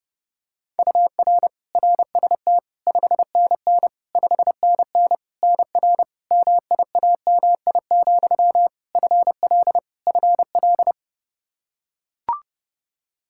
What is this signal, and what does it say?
Morse code, heard as a keyed tone, about 30 wpm, 700 Hz. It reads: UR RST 5NN 5NN NR MIAMI, FL FL
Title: K1USN SST QSO Practice 30wpm